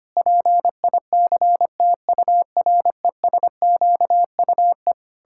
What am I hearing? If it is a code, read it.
Morse code, heard as a keyed tone, about 25 wpm, 700 Hz. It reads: PICTURESQUE
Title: Pangram Challenge v2 - 25wpm